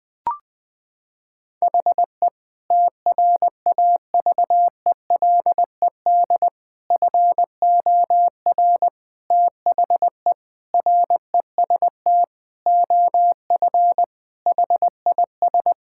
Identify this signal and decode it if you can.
Morse code, heard as a keyed tone, about 20 wpm, 700 Hz. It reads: HE TRAVELED FOR THE REST OF HIS
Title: Sentences with 5-Word Phrases - 20wpm